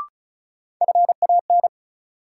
Morse code, heard as a keyed tone, about 36 wpm, 700 Hz. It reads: FAN
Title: Sentences from Top 1000 Words - Review Part 4 - New Word Drill - 35wpm